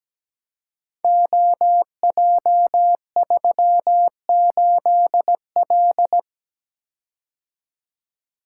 Morse code, heard as a keyed tone, about 17 wpm, 700 Hz. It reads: OJ38L
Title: Code Groups - 5 Letter:Numbers - 17wpm